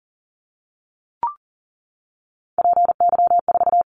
Morse code, heard as a keyed tone, about 40 wpm, 700 Hz. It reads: PY4
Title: Callsign Prefixes for Top 100 DXCC - 40wpm